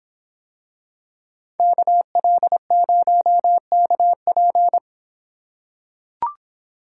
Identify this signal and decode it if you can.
Morse code, heard as a keyed tone, about 26 wpm, 700 Hz. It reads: KL0KP